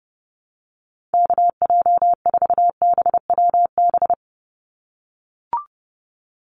Morse code, heard as a keyed tone, about 29 wpm, 700 Hz. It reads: KJ4BWB
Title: USA Call Signs 30wpm